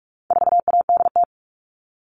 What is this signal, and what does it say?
Morse code, heard as a keyed tone, about 44 wpm, 700 Hz. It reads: 4ADT
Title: All Cut Numbers - 4-Digit 45wpm